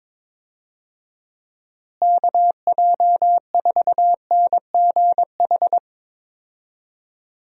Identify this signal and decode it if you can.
Morse code, heard as a keyed tone, about 22 wpm, 700 Hz. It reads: KJ4NGH
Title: USA Call Signs - 6 Characters - Repeated 2x - 22wpm